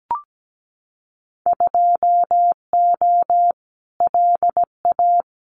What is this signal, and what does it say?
Morse code, heard as a keyed tone, about 17 wpm, 700 Hz. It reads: 2O LA
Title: Winter Field Day Exchanges 17wpm